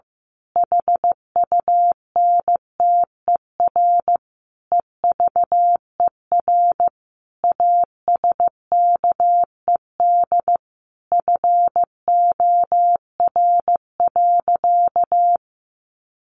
Morse code, heard as a keyed tone, about 15 wpm, 700 Hz. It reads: HUNTER EVER ASKED FOR.
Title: A Sound Of Thunder 15wpm